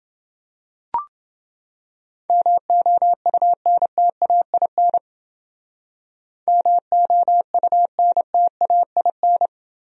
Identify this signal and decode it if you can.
Morse code, heard as a keyed tone, about 29 wpm, 700 Hz. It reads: MOUNTAIN MOUNTAIN